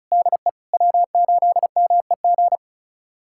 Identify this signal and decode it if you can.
Morse code, heard as a keyed tone, about 36 wpm, 690 Hz. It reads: DE W8MEG
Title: USA Callsign DE Callsign - 5 Characters - 35wpm